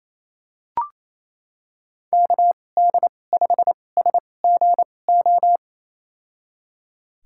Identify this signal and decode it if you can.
Morse code, heard as a keyed tone, about 28 wpm, spaced as for 20 wpm, 700 Hz. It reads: KD5SGO
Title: USA Call Signs - 6 Characters - 2x Character Spacing - 28wpm